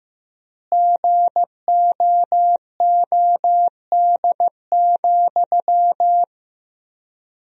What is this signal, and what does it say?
Morse code, heard as a keyed tone, about 16 wpm, 700 Hz. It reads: GOOD,
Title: 4x Word-Spacing - Sentences from Top 200 Words - 15wpm